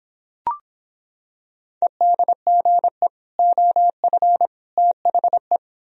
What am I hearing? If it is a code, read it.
Morse code, heard as a keyed tone, about 26 wpm, 700 Hz. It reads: EDGE OF THE